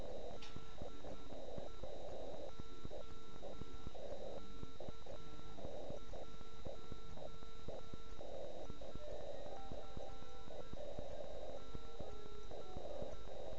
Morse code, heard as a keyed tone, about 9 wpm, 1340 Hz. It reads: DE OK1IF GE